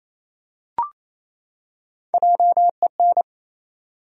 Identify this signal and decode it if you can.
Morse code, heard as a keyed tone, about 28 wpm, 700 Hz. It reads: JEN
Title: USA Common Names - 28wpm